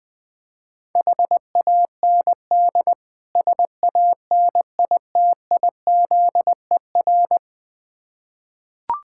Morse code, heard as a keyed tone, about 20 wpm, 690 Hz. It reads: HAND SANITIZER
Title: Difficult Words with Hints - 20wpm